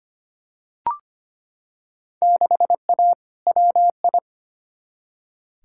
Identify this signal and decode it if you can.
Morse code, heard as a keyed tone, about 25 wpm, 700 Hz. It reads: 6A WI
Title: Field Day Exchanges - 25wpm